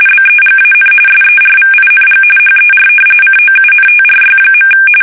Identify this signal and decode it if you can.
RTTY, about 45.45 baud. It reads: BEGAN SERVING HER COUNTRY.
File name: fsk850 / battleshipnj